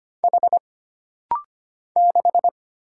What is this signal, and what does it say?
Morse code, heard as a keyed tone, about 25 wpm, 700 Hz. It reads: H 6
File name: Single Character (Letter:Number:Punctuation:Prosign) - Rapid-Fire - 25wpm